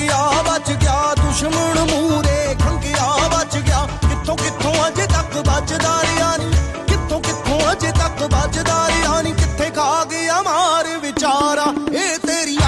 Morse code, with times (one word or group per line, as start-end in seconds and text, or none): none